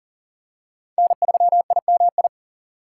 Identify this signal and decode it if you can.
Morse code, heard as a keyed tone, about 40 wpm, 700 Hz. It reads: N3IMI
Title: USA Call Signs - 5 Characters - 3x Character Spacing - 40wpm